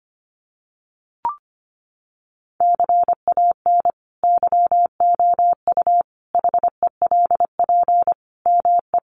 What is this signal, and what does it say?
Morse code, heard as a keyed tone, about 25 wpm, 700 Hz. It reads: CAN YOU HELP ME
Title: Sentences from Top 200 Words 25wpm